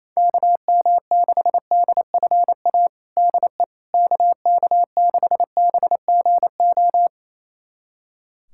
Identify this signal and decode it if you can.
Morse code, heard as a keyed tone, about 28 wpm, 700 Hz. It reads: KM6DFA DE KK6BGO
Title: USA Callsign DE Callsign - 6 Characters - 28wpm